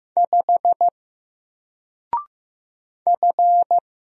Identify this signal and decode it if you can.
Morse code, heard as a keyed tone, about 15 wpm, 700 Hz. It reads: 5 F
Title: Single Letter-Number Alt 15wpm